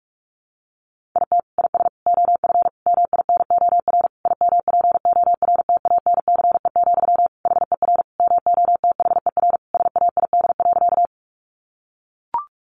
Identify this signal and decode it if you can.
Morse code, heard as a keyed tone, about 44 wpm, 700 Hz. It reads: IT IS OF MINOR IMPORTANCE, HER MOTHER SAID.
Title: Sentences from Top 1500 Words 45wpm